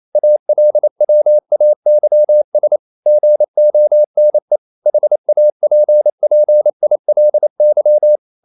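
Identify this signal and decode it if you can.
Morse code, heard as a keyed tone, about 28 wpm, 600 Hz. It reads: ALWAYS GONE HAPPILY
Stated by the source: Morse code practice at 28 wpm, text of Ken Follett's novel Never in English, from CW med Gnister.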